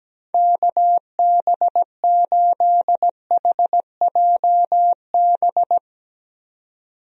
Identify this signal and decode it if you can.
Morse code, heard as a keyed tone, about 17 wpm, 700 Hz. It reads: KB8HJB
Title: USA Call Signs - 6 Characters - 17wpm